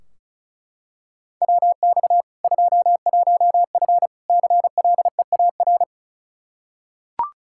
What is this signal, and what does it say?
Morse code, heard as a keyed tone, about 36 wpm, 700 Hz. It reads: WX 21F CLEAR
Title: Practice QSO - Exchange 2 - Info - 35wpm